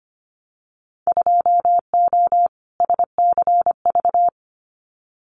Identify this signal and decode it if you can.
Morse code, heard as a keyed tone, about 25 wpm, 700 Hz. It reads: 2O SCV
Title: Winter Field Day Exchanges Encore 25wpm